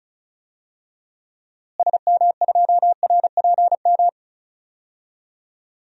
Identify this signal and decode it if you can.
Morse code, heard as a keyed tone, about 36 wpm, 700 Hz. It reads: SM2RPM
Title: DXCC-100 Call Signs - 6 Characters - Repeated 3x - 35wpm